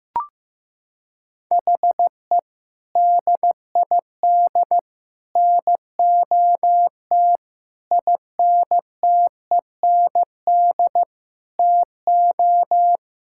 Morse code, heard as a keyed tone, about 15 wpm, 700 Hz. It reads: HE DID NOT INTEND TO
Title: Sentences from Top 1600 Words 15wpm